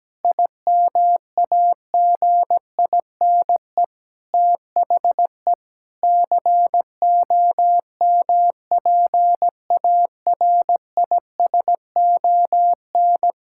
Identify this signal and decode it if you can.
Morse code, heard as a keyed tone, about 17 wpm, 700 Hz. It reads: IMAGINE THE COMPARISON
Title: Sentences from Top 1600 Words 17wpm